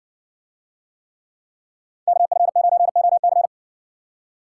Morse code, heard as a keyed tone, about 60 wpm, 700 Hz. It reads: KF9OC